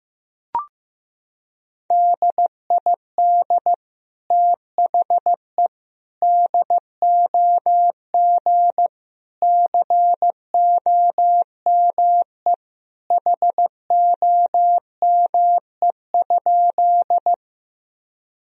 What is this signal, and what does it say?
Morse code, heard as a keyed tone, about 15 wpm, 700 Hz. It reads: DID THE DOG COME HOME?
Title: Sentences from Top 1000 Words - Review Part 2 - 15wpm